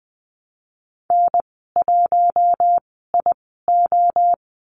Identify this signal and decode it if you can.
Morse code, heard as a keyed tone, about 20 wpm, spaced as for 15 wpm, 700 Hz. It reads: N1IO
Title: 055 Lesson 10g - Letter r - Call Signs 20wpm